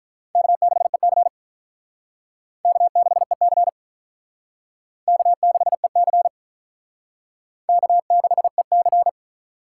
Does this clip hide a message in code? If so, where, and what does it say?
Yes: Morse code from 0.4 to 9.1 s, heard as a keyed tone, about 45 wpm, 700 Hz. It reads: K6EC K6EC K6EC K6EC